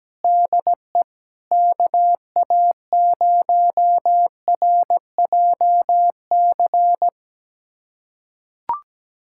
Morse code, heard as a keyed tone, about 17 wpm, 700 Hz. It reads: DE KA0RJC
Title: USA Callsign DE Callsign - 6 Characters - 17wpm